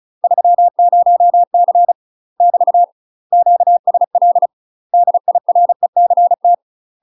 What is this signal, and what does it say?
Morse code, heard as a keyed tone, about 36 wpm, 700 Hz. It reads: 30C <BT> QSL DIRECT
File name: UK MASTER QSOs TRAINING / 35WPM / UK MASTER QSOs 0013